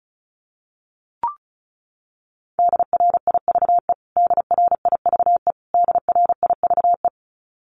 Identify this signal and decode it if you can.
Morse code, heard as a keyed tone, about 34 wpm, 700 Hz. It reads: DRIVE DRIVE DRIVE
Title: Top 500 Words Repeated 35wpm